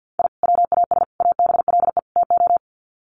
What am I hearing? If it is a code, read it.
Morse code, heard as a keyed tone, about 50 wpm, 700 Hz. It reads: I WAS ABLE TO
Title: Sets of 4 Words 50wpm